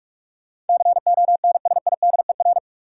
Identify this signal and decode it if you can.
Morse code, heard as a keyed tone, about 42 wpm, 700 Hz. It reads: CONSIDER
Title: Sentences from Top 600 Words - New Word Drill - 45wpm